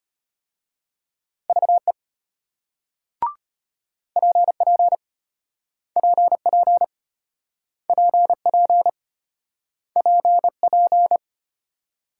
Morse code, heard as a keyed tone, about 34 wpm, 700 Hz. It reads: VE PP PP PP PP